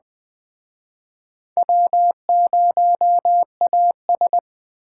Morse code, heard as a keyed tone, about 21 wpm, 700 Hz. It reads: W0AS